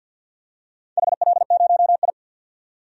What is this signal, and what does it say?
Morse code, heard as a keyed tone, about 52 wpm, 700 Hz. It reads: HL0I
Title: DXCC-100 Call Signs - 4 Characters - 50wpm